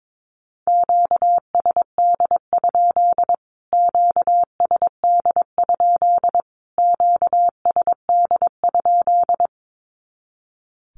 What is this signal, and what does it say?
Morse code, heard as a keyed tone, about 22 wpm, 700 Hz. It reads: QSD? QSD? QSD?
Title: Q-Codes Repeated 3x 22wpm